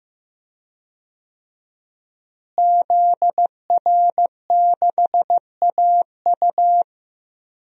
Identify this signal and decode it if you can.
Morse code, heard as a keyed tone, about 15 wpm, 700 Hz. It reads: ZR6AU
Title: DXCC-100 Call Signs - 5 Characters - Repeated 3x - 15wpm